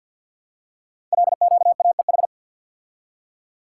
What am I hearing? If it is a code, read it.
Morse code, heard as a keyed tone, about 51 wpm, 700 Hz. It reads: LQAEH
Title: Code Groups - 5 Letters - 50wpm